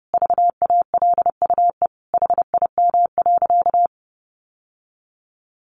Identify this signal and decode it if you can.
Morse code, heard as a keyed tone, about 30 wpm, 700 Hz. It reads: VALUE HIM.
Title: Sentences from Top 600 Words 30wpm